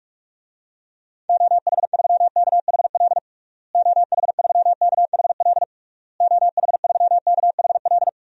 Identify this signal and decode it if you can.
Morse code, heard as a keyed tone, about 46 wpm, 700 Hz. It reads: OH3KHL OH3KHL OH3KHL
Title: DXCC-100 Call Signs - 6 Characters - Repeated 3x - 45wpm